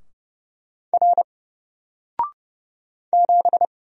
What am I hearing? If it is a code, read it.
Morse code, heard as a keyed tone, about 30 wpm, 700 Hz. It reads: R 7
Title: Single Letter-Number Alt 30wpm